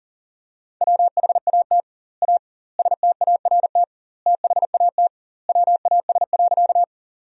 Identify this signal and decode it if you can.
Morse code, heard as a keyed tone, about 40 wpm, 700 Hz. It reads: WHAT A START THAT WAS.
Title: 2x Word-Spacing - Sentences from Top 300 Words - 40wpm